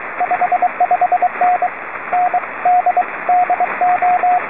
Morse code, heard as a keyed tone, about 23 wpm, 720 Hz, with radio noise. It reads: 55N N D DO